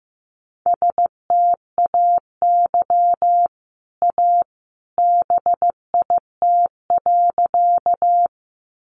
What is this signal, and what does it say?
Morse code, heard as a keyed tone, about 15 wpm, 700 Hz. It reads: STAY A BIT.
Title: Sentences from Top 900 Words 15wpm